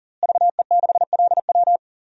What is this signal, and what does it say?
Morse code, heard as a keyed tone, about 37 wpm, 700 Hz. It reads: VE6LW
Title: DXCC-100 Call Signs - 5 Characters - Repeated 3x - 40wpm